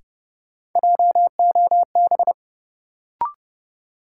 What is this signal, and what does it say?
Morse code, heard as a keyed tone, about 32 wpm, 700 Hz. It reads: JOB
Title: Sentences from Top 600 Words - New Word Drill - 30wpm